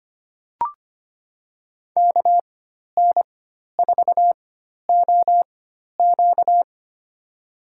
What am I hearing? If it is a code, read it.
Morse code, heard as a keyed tone, about 25 wpm, spaced as for 12 wpm, 700 Hz. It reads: KN4OQ
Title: USA Call Signs - 5 Characters - 4x Character Spacing - 25wpm